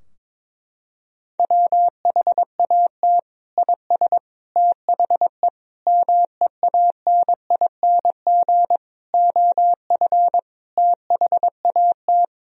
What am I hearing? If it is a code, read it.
Morse code, heard as a keyed tone, about 22 wpm, 700 Hz. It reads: WHAT IS THE MEANING OF THAT
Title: Sentences from Top 1000 Words - Review Part 4 - 22wpm